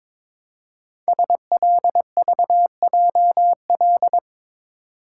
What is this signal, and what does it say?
Morse code, heard as a keyed tone, about 22 wpm, 700 Hz. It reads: SLVJL
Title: Code Groups - 5 Letters - 22wpm